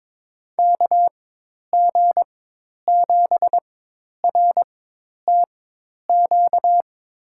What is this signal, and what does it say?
Morse code, heard as a keyed tone, about 22 wpm, spaced as for 10 wpm, 700 Hz. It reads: KG7RTQ
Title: USA Call Signs - 6 Characters - 4x Character Spacing - 22wpm